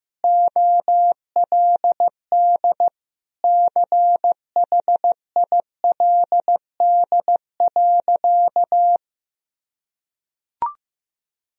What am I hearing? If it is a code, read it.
Morse code, heard as a keyed tone, about 15 wpm, 700 Hz. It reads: OLD CHILD.
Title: Sentences from Top 1000 Words - Review Part 2 - 15wpm